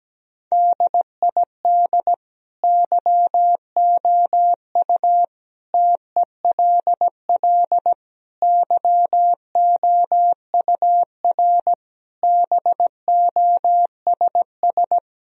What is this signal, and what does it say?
Morse code, heard as a keyed tone, about 17 wpm, 700 Hz. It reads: DID YOU TELL YOUR BOSS
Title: Sentences from Top 1000 Words - Review Part 4 - 17wpm